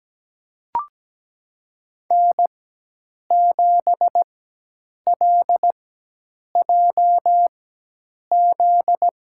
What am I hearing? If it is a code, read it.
Morse code, heard as a keyed tone, about 17 wpm, spaced as for 8 wpm, 700 Hz. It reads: N7LJZ
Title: USA Call Signs - 5 Characters - 4x Character Spacing - 17wpm